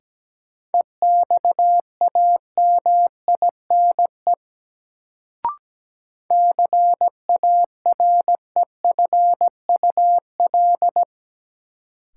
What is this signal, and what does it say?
Morse code, heard as a keyed tone, about 17 wpm, 700 Hz. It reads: EXAMINE CAREFUL
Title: Sentences from Top 1600 Words - New Word Drill Alt - 17wpm